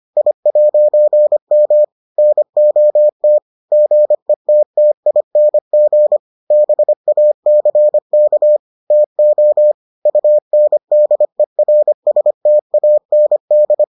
Morse code, heard as a keyed tone, about 25 wpm, 600 Hz. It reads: I'M NOT GETTING BACK TO UNDERSTAND